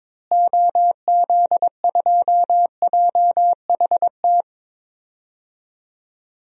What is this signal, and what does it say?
Morse code, heard as a keyed tone, about 22 wpm, 700 Hz. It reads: OZ2JHT
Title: DXCC-100 Call Signs - 6 Characters - Repeated 2x - 22wpm